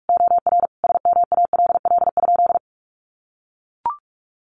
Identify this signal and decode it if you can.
Morse code, heard as a keyed tone, about 44 wpm, 700 Hz. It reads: OR SMALL?
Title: Sentences from Top 200 Words 45wpm